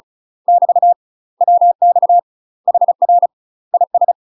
Morse code, heard as a keyed tone, about 33 wpm, 700 Hz. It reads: <BT> WX HR IS
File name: UK MASTER QSOs TRAINING / 35WPM / UK MASTER QSOs 0009